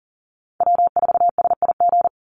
Morse code, heard as a keyed tone, about 39 wpm, 700 Hz. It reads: W4SIG
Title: Call Signs - 40wpm